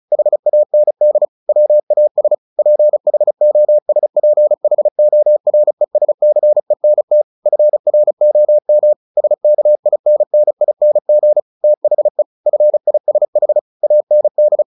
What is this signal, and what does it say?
Morse code, heard as a keyed tone, about 34 wpm, 600 Hz. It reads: HAND WAS PHOSPHORESCENT FROM SKINNING THE FISH AND
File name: the old man and the sea 2 35wpm